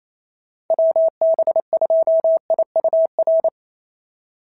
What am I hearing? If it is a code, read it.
Morse code, heard as a keyed tone, about 27 wpm, 650 Hz. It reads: WB2IUR
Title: SOTA QSO Practice 28wpm